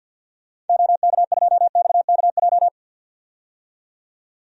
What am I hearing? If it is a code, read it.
Morse code, heard as a keyed tone, about 50 wpm, 700 Hz. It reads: OK2XKJ
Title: DXCC-100 Call Signs - 6 Characters - 50wpm